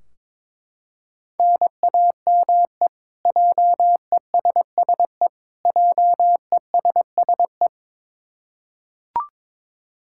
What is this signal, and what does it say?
Morse code, heard as a keyed tone, about 22 wpm, 700 Hz. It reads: NAME JESSE JESSE
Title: K1USN SST QSO Practice 22wpm